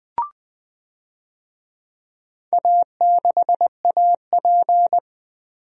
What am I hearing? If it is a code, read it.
Morse code, heard as a keyed tone, about 21 wpm, 700 Hz. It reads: A6AP
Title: DXCC-100 Call Signs - 4 Characters - 20wpm